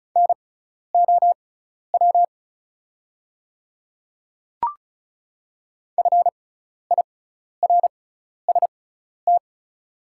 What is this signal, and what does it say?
Morse code, heard as a keyed tone, about 34 wpm, spaced as for 12 wpm, 700 Hz. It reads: NOW FIRST
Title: 6x Character Spacing - Top 200 Words - 35wpm